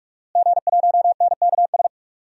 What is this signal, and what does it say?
Morse code, heard as a keyed tone, about 46 wpm, 700 Hz. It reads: G1NKS
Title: DXCC-100 Call Signs - 5 Characters - Repeated 3x - 45wpm